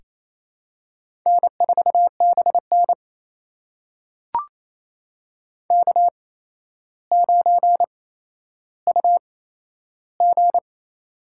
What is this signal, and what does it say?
Morse code, heard as a keyed tone, about 28 wpm, spaced as for 12 wpm, 700 Hz. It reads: N4BN K9UG
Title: USA Call Signs - 4 Characters - 8x Character Spacing - 28wpm